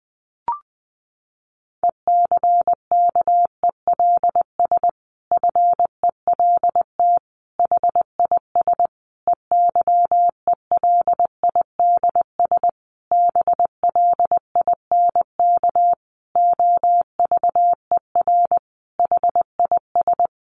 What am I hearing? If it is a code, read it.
Morse code, heard as a keyed tone, about 20 wpm, 700 Hz. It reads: ECKELS FELT HIS EYELIDS BLINK OVER HIS
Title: A Sound Of Thunder 20wpm